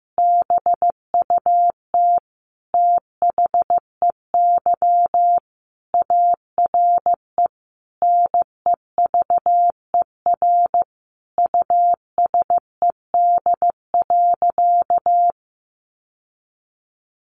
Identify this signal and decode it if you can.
Morse code, heard as a keyed tone, about 15 wpm, 700 Hz. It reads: BUT THEY ARE NEVER USED.
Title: Sentences from Top 600 Words 15wpm